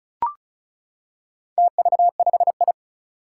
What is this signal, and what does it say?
Morse code, heard as a keyed tone, about 36 wpm, 700 Hz. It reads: TV5I
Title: Four Challenging Letter-Number Combinations 35wpm